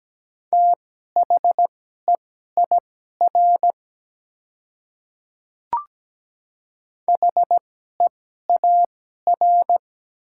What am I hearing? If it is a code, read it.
Morse code, heard as a keyed tone, about 17 wpm, spaced as for 12 wpm, 700 Hz. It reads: THEIR HEAR
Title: 2x Character Spacing - Top 300 Words - 17wpm